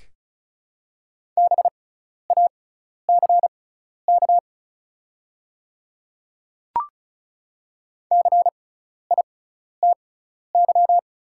Morse code, heard as a keyed tone, about 36 wpm, spaced as for 12 wpm, 700 Hz. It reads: BACK CITY
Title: 6x Character Spacing - Top 300 Words - 35wpm